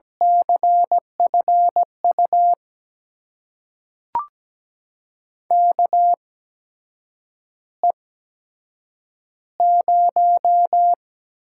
Morse code, heard as a keyed tone, about 17 wpm, spaced as for 7 wpm, 700 Hz. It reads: CFU KE0
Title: USA Call Signs - 6 Characters - 8x Character Spacing - 17wpm